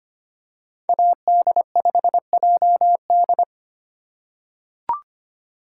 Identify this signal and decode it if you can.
Morse code, heard as a keyed tone, about 25 wpm, 700 Hz. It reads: AD5JD
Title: USA Call Signs - 5 Characters - 3x Character Spacing - 25wpm